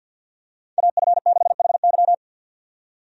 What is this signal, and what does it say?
Morse code, heard as a keyed tone, about 52 wpm, 700 Hz. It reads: AF6HY
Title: USA Call Signs - 5 Characters - 50wpm